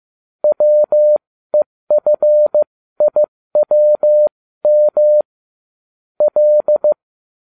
Morse code, heard as a keyed tone, about 15 wpm, spaced as for 13 wpm, 600 Hz. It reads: WEFIWM L